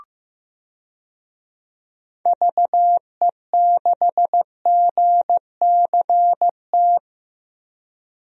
Morse code, heard as a keyed tone, about 15 wpm, 700 Hz. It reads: VE6GCT
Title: DXCC-100 Call Signs - 6 Characters - Repeated 2x - 15wpm